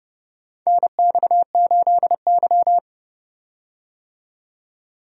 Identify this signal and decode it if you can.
Morse code, heard as a keyed tone, about 30 wpm, 700 Hz. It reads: NX8Y
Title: USA Call Signs - 4 Characters - 30wpm